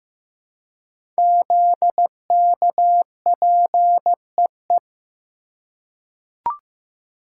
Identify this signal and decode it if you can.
Morse code, heard as a keyed tone, about 15 wpm, 700 Hz. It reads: ZKPEE
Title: Code Groups - 5 Letters - 15wpm